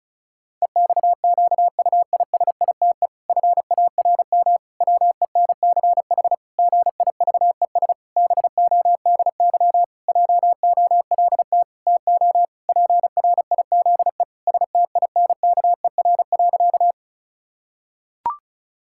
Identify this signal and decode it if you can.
Morse code, heard as a keyed tone, about 34 wpm, 700 Hz. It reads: EXQUISITE FARM WENCH GIVES BODY JOLT TO PRIZE STINKER.
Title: Pangram Challenge v2 - 35wpm